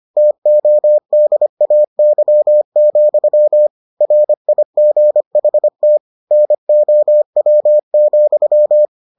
Morse code, heard as a keyed tone, about 25 wpm, 600 Hz. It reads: TODAY, RIGHT NOW,